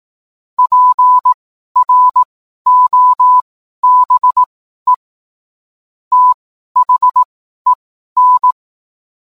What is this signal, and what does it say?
Morse code, heard as a keyed tone, about 18 wpm, spaced as for 12 wpm, 1000 Hz. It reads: PROBE THEN